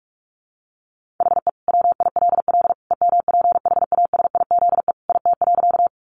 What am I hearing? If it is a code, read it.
Morse code, heard as a keyed tone, about 44 wpm, 700 Hz. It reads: HE WILL EMPHASIZE IT.
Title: Sentences from Top 1500 Words 45wpm